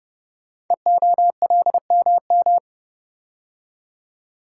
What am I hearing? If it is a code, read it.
Morse code, heard as a keyed tone, about 30 wpm, 700 Hz. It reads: EOLMM
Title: Code Groups - 5 Letters - 30wpm